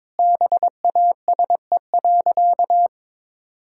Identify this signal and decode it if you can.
Morse code, heard as a keyed tone, about 22 wpm, 700 Hz. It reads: BASE.
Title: Sentences from Top 1600 Words 22wpm